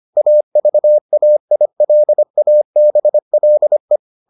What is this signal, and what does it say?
Morse code, heard as a keyed tone, about 25 wpm, 600 Hz. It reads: AVAILABLE